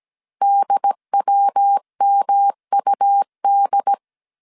Morse code, heard as a keyed tone, about 17 wpm, 800 Hz. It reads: DWMUD
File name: CW / DET / 1 6 1500C S A 16wpm 2635